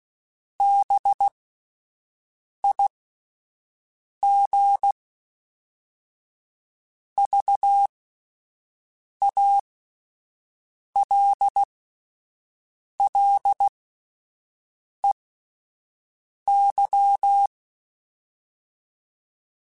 Morse code, heard as a keyed tone, about 16 wpm, spaced as for 5 wpm, 800 Hz. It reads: BIGVALLEY